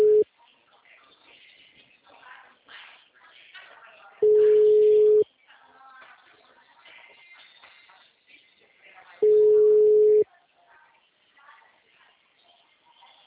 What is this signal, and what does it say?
Ringback tone (Europe, 425 Hz).